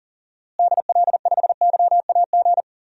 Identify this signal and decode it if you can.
Morse code, heard as a keyed tone, about 40 wpm, 700 Hz. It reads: DL5YAG
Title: DXCC-100 Call Signs - 6 Characters - Repeated 3x - 40wpm